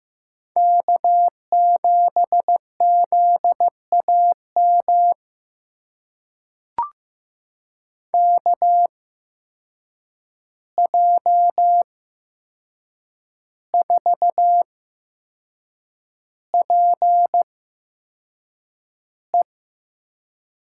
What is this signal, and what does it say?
Morse code, heard as a keyed tone, about 15 wpm, spaced as for 6 wpm, 700 Hz. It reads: K7ZAM KJ4PE
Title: USA Call Signs - 5 Characters - 8x Character Spacing - 15wpm